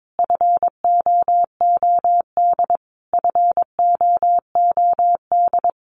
Morse code, heard as a keyed tone, about 22 wpm, 700 Hz. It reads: FOOD FOOD
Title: Top 500 Words Repeated 22wpm